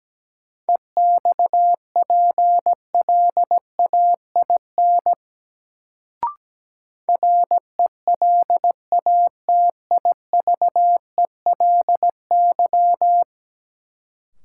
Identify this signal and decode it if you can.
Morse code, heard as a keyed tone, about 17 wpm, 700 Hz. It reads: EXPLAIN RELATIVELY
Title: Sentences from Top 1100 Words - New Word Drill Alt - 17wpm